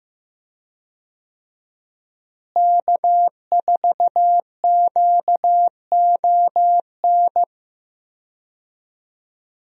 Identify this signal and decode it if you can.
Morse code, heard as a keyed tone, about 15 wpm, 700 Hz. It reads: K4QON
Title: USA Call Signs - 5 Characters - Repeated 3x - 15wpm